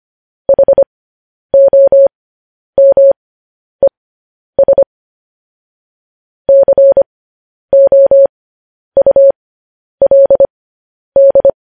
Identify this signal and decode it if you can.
Morse code, heard as a keyed tone, about 25 wpm, spaced as for 10 wpm, 550 Hz. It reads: HOMES COULD
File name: podcast / News headlines-1x-25x10